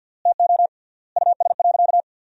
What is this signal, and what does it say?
Morse code, heard as a keyed tone, about 50 wpm, 700 Hz. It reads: TO US.
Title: 3x Word-Spacing - Sentences from Top 500 Words - 50wpm